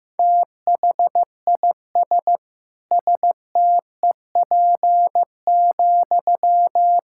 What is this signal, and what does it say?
Morse code, heard as a keyed tone, about 15 wpm, 700 Hz. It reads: THIS STEP,
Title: Sentences from Top 1000 Words - Review Part 6 - 15wpm